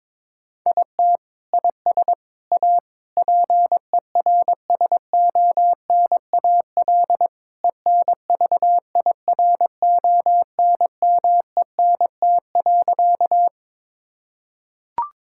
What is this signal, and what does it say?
Morse code, heard as a keyed tone, about 22 wpm, 700 Hz. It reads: IT IS A PERSONAL ENVIRONMENT.
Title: Sentences from Top 1000 Words - Review Part 5 - 22wpm